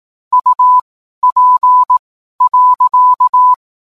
Morse code, heard as a keyed tone, about 18 wpm, spaced as for 12 wpm, 1000 Hz. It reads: UP.